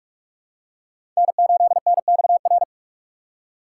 Morse code, heard as a keyed tone, about 44 wpm, 700 Hz. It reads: N8NXR
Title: USA Call Signs - 5 Characters - 2x Character Spacing - 45wpm